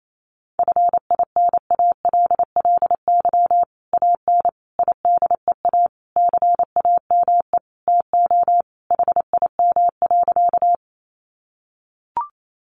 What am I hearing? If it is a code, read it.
Morse code, heard as a keyed tone, about 28 wpm, 700 Hz. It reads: FINALLY AN IDEA CAME TO HIM.
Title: Sentences from Top 1000 Words 28wpm